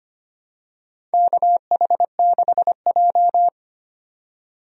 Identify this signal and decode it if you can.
Morse code, heard as a keyed tone, about 25 wpm, 700 Hz. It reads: KH6J
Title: DXCC-100 Call Signs - 4 Characters - 25wpm